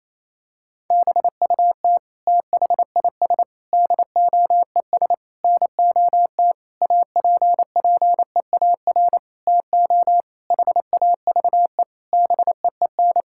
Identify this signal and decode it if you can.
Morse code, heard as a keyed tone, about 28 wpm, 700 Hz. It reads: BUT THIS DOES NOT APPEAR TO HAVE BEEN
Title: Sentences from Top 1000 Words - Review Part 6 - 28wpm